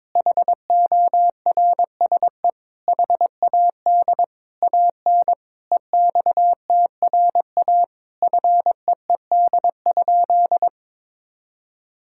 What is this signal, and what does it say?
Morse code, heard as a keyed tone, about 22 wpm, 700 Hz. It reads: HORSE HAD AN EXTRA FEED?
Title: Sentences from Top 1600 Words 22wpm